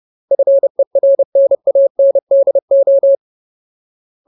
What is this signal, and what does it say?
Morse code, heard as a keyed tone, about 30 wpm, 550 Hz. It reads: FERNANDO
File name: Ultimate Word List / 30WPM Ultimate